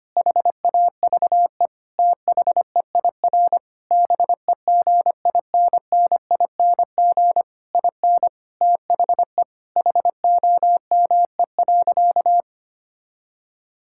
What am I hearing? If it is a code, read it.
Morse code, heard as a keyed tone, about 25 wpm, 700 Hz. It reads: HAVE THEIR BEGINNING IN THE HOME.
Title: Sentences from Top 1000 Words - Review Part 6 - 25wpm